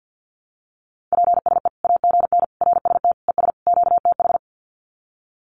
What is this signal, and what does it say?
Morse code, heard as a keyed tone, about 50 wpm, 700 Hz. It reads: PSE AGN RST ES QTH
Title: Common QSO Phrases 50wpm